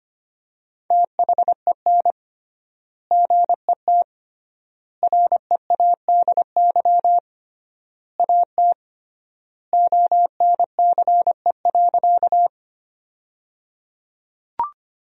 Morse code, heard as a keyed tone, about 25 wpm, 700 Hz. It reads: THEN GET READY AT ONCE.
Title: 3x Word-Spacing - Sentences from Top 400 Words - 25wpm